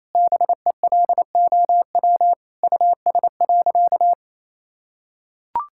Morse code, heard as a keyed tone, about 28 wpm, 700 Hz. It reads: BELOW US.
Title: Sentences from Top 1000 Words - Review Part 5 - 28wpm